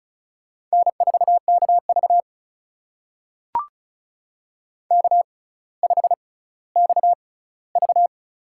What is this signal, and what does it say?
Morse code, heard as a keyed tone, about 34 wpm, spaced as for 18 wpm, 700 Hz. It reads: N4KV K5XV
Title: USA Call Signs - 4 Characters - 6x Character Spacing - 35wpm